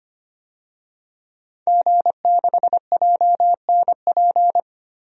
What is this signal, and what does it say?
Morse code, heard as a keyed tone, about 25 wpm, 700 Hz. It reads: G6JNP
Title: DXCC-100 Call Signs - 5 Characters - 25wpm